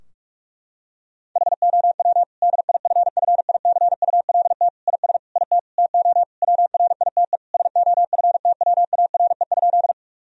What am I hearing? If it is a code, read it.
Morse code, heard as a keyed tone, about 41 wpm, 700 Hz. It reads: HOW DIFFICULT IS IT TO WRITE SOFTWARE?
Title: Sentences from Top 1000 Words - Review Part 3 - 45wpm